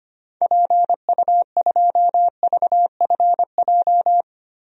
Morse code, heard as a keyed tone, about 25 wpm, 700 Hz. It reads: PU2VFJ
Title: DXCC-100 Call Signs - 6 Characters - Repeated 2x - 25wpm